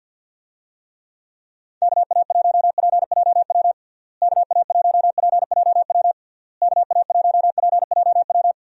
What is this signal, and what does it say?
Morse code, heard as a keyed tone, about 52 wpm, 700 Hz. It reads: KA1PJW KA1PJW KA1PJW
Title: USA Call Signs - 6 Characters - Repeated 3x - 50wpm